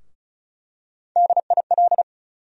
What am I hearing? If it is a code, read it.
Morse code, heard as a keyed tone, about 34 wpm, 700 Hz. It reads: DIL
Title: USA Call Sign Postfixes - 3 Letters 35wpm